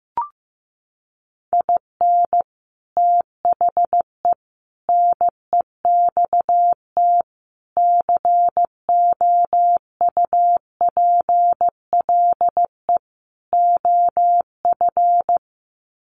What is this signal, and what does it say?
Morse code, heard as a keyed tone, about 15 wpm, 700 Hz. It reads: IN THE NEXT COUPLE OF
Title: Sets of 5 Words Encore2 15wpm